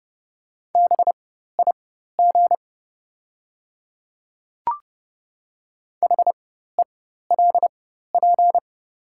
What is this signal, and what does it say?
Morse code, heard as a keyed tone, about 30 wpm, spaced as for 14 wpm, 700 Hz. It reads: BIG HELP
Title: 4x Character Spacing - Top 300 Words - 30wpm